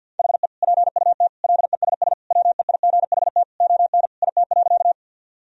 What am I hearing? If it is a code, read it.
Morse code, heard as a keyed tone, about 51 wpm, 700 Hz. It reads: HE PUT LESS WEIGHT ON IT.
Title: Sentences from Top 1000 Words - Review Part 3 - 50wpm